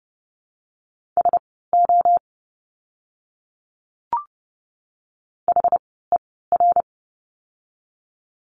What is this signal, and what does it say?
Morse code, heard as a keyed tone, about 30 wpm, spaced as for 17 wpm, 700 Hz. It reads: SO HER
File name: course / 30:10wpm / 230 Lesson 37f - Letter x - Words 30wpm